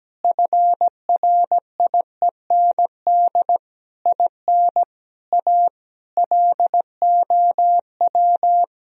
Morse code, heard as a keyed tone, about 17 wpm, 700 Hz. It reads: FRIEND IN A LOW
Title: Sentences from Top 1000 Words - Review Part 1 - 17wpm